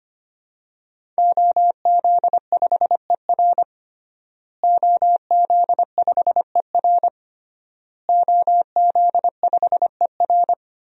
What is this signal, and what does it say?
Morse code, heard as a keyed tone, about 25 wpm, 700 Hz. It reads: OZ5ER OZ5ER OZ5ER
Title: DXCC-100 Call Signs - 5 Characters - Repeated 3x - 25wpm